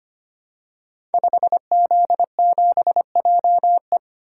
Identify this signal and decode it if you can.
Morse code, heard as a keyed tone, about 25 wpm, 700 Hz. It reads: 5Z7JE
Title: Code Groups - 5 Letter:Numbers - 25wpm